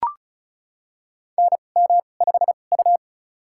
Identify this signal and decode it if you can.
Morse code, heard as a keyed tone, about 36 wpm, spaced as for 25 wpm, 700 Hz. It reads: NM5U
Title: USA Call Signs - 2x Character Spacing - 35wpm